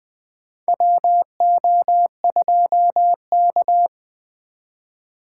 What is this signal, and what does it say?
Morse code, heard as a keyed tone, about 20 wpm, 700 Hz. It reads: WO2K
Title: USA Call Signs - 4x Character Spacing - 20wpm